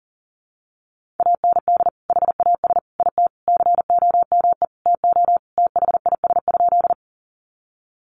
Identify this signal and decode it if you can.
Morse code, heard as a keyed tone, about 40 wpm, 700 Hz. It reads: AND HAS IT COME TO THIS?
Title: Sentences from Top 100 Words 40wpm